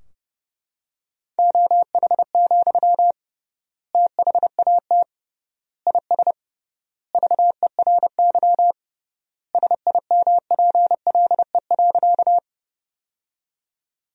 Morse code, heard as a keyed tone, about 30 wpm, 700 Hz. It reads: OH, THAT IS VERY SIMPLE.
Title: 3x Word-Spacing - Sentences from Top 500 Words - 30wpm